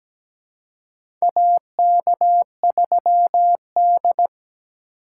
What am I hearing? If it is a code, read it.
Morse code, heard as a keyed tone, about 17 wpm, 700 Hz. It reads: AK3D
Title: USA Call Signs - 4 Characters - 3x Character Spacing - 17wpm